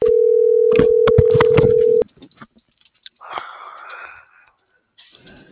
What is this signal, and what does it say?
Ringback tone (North America), 440 + 480 Hz.
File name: CENTRAL MULTIMODAL / out viridiana.juarez - .83174